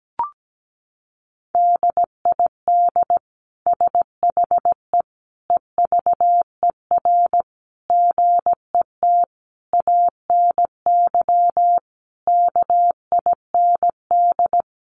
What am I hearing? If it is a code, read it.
Morse code, heard as a keyed tone, about 17 wpm, 700 Hz. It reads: DID SHE EVER GET ANY KIND
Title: Sentences from Top 1100 Words 17wpm